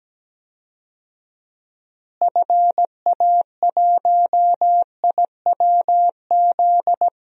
Morse code, heard as a keyed tone, about 17 wpm, 700 Hz. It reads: FA1IWZ
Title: DXCC-100 Call Signs - 6 Characters - Repeated 3x - 17wpm